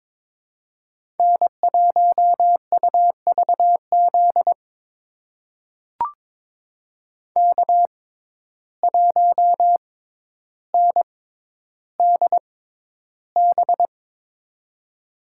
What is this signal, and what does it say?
Morse code, heard as a keyed tone, about 21 wpm, spaced as for 11 wpm, 700 Hz. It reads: N1UVZ K1NDB
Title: USA Call Signs - 5 Characters - 6x Character Spacing - 22wpm